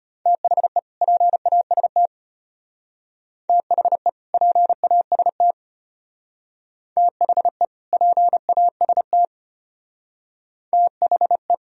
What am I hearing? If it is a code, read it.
Morse code, heard as a keyed tone, about 33 wpm, 700 Hz. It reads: THE PAST THE PAST THE PAST THE